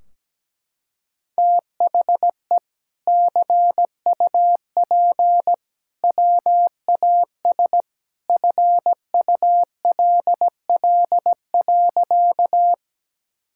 Morse code, heard as a keyed tone, about 17 wpm, 700 Hz. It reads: THE CUP WAS FULL.
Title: Sentences from Top 1600 Words 17wpm